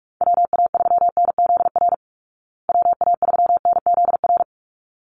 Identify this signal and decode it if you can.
Morse code, heard as a keyed tone, about 44 wpm, 700 Hz. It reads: WA3NZR WA3NZR
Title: Call Signs Repeated 3x - 45wpm